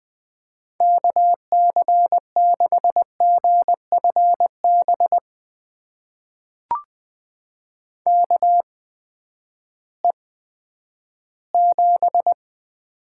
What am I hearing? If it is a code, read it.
Morse code, heard as a keyed tone, about 20 wpm, spaced as for 11 wpm, 700 Hz. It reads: KC6GFB KE7
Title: USA Call Signs - 6 Characters - 8x Character Spacing - 20wpm